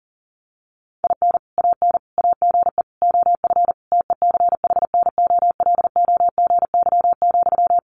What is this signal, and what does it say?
Morse code, heard as a keyed tone, about 40 wpm, 700 Hz. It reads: IN AN AGE OF TECHNOLOGY,
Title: Sentences from Top 1200 Words 40wpm